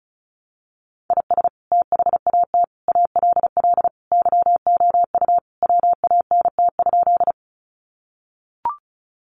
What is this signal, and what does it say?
Morse code, heard as a keyed tone, about 34 wpm, 700 Hz. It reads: IS THAT ALL YOU WANT?
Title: Sentences from Top 200 Words 35wpm